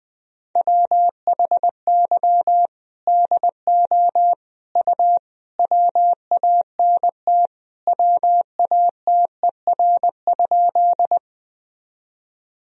Morse code, heard as a keyed tone, about 20 wpm, 700 Hz. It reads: WHY DO U WANT WATER?
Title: Sentences from Top 200 Words and Abbreviations 20wpm